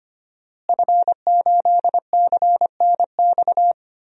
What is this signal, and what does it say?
Morse code, heard as a keyed tone, about 25 wpm, 700 Hz. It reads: F8CNX
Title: DXCC-100 Call Signs - 5 Characters - Repeated 2x - 25wpm